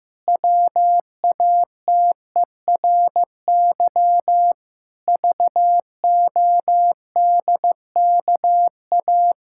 Morse code, heard as a keyed tone, about 15 wpm, 700 Hz. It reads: WATERY VODKA
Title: Pangram Challenge v2 - 15wpm